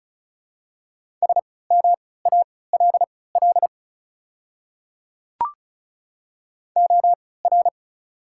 Morse code, heard as a keyed tone, about 36 wpm, spaced as for 20 wpm, 700 Hz. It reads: SMALL OR
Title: 3x Character Spacing - Top 200 Words - 35wpm